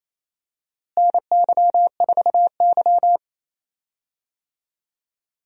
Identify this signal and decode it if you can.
Morse code, heard as a keyed tone, about 28 wpm, 700 Hz. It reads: NY4Y
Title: USA Call Signs - 4 Characters - Repeated 3x - 28wpm